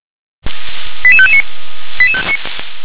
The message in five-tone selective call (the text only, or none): 88559